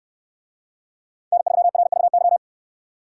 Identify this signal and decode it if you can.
Morse code, heard as a keyed tone, about 69 wpm, 700 Hz. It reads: N3RVY